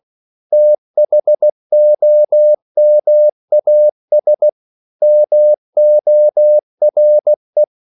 Morse code, heard as a keyed tone, about 16 wpm, 600 Hz. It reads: THOMAS MORE
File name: CW / Practice / EnglishQuotations0116